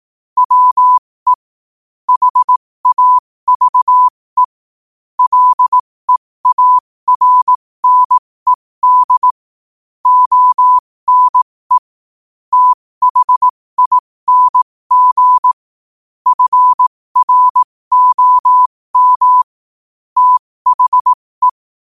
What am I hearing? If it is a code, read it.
Morse code, heard as a keyed tone, about 18 wpm, spaced as for 15 wpm, 1000 Hz. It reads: WE HAVE LEARNED ONE THING FROM THE
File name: qotdfiles / 15wpm